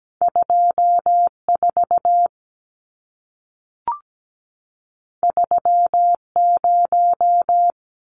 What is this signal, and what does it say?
Morse code, heard as a keyed tone, about 17 wpm, 700 Hz. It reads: 24 30
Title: Numbers - 2-Digit 17wpm